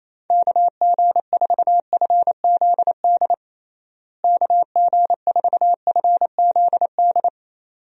Morse code, heard as a keyed tone, about 27 wpm, 700 Hz. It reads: KG4FZD KG4FZD
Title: USA Call Signs - 6 Characters - Repeated 2x - 28wpm